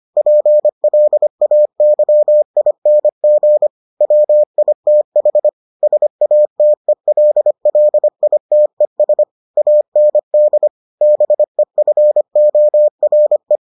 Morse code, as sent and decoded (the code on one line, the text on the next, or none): .--. .-.. .- -.-- .. -. --. / .-- .. - .... / ... .- - . .-.. .-.. .. - . ... / .- -. -.. / -... ..-. --- .-. .
PLAYING WITH SATELLITES AND BEFORE